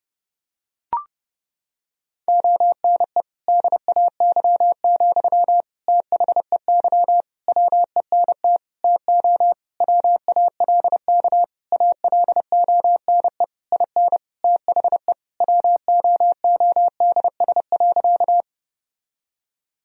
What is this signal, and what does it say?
Morse code, heard as a keyed tone, about 30 wpm, 700 Hz. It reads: ONE DAY, THEY WENT TO WALK ALONE IN THE WOODS.
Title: Sentences from Top 1000 Words - Review Part 2 - 30wpm